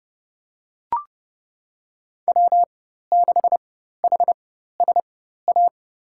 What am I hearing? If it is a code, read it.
Morse code, heard as a keyed tone, about 30 wpm, spaced as for 14 wpm, 700 Hz. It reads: W6HSA
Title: USA Call Signs - 5 Characters - 4x Character Spacing - 30wpm